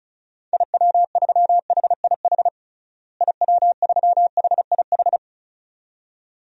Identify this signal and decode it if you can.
Morse code, heard as a keyed tone, about 36 wpm, 700 Hz. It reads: IW3HIH IW3HIH
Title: DXCC-100 Call Signs - 6 Characters - Repeated 2x - 35wpm